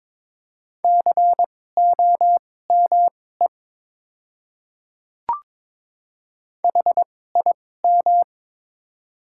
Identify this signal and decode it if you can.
Morse code, heard as a keyed tone, about 22 wpm, spaced as for 16 wpm, 690 Hz. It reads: COME HIM
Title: 2x Character Spacing - Top 100 Words - 22wpm